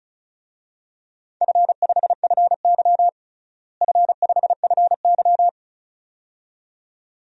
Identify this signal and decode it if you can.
Morse code, heard as a keyed tone, about 34 wpm, 700 Hz. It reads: F5FY F5FY
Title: DXCC-100 Call Signs - 4 Characters - Repeated 2x - 35wpm